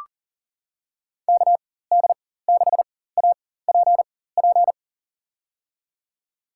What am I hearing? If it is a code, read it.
Morse code, heard as a keyed tone, about 40 wpm, spaced as for 19 wpm, 700 Hz. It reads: KD6APP
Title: USA Call Signs - 6 Characters - 4x Character Spacing - 40wpm